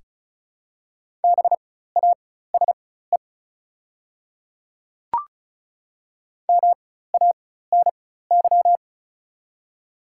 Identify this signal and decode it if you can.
Morse code, heard as a keyed tone, about 36 wpm, spaced as for 16 wpm, 690 Hz. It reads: BASE MANY
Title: 4x Character Spacing - Top 300 Words - 35wpm